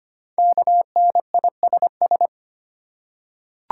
Morse code, heard as a keyed tone, about 25 wpm, 700 Hz. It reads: KNISS
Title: Code Groups - 5 Letters - 25wpm